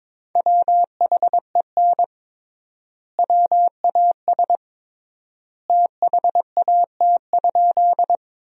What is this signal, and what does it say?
Morse code, heard as a keyed tone, about 22 wpm, 700 Hz. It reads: WHEN WAS THAT?
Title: 3x Word-Spacing - Sentences from Top 100 Words - 22wpm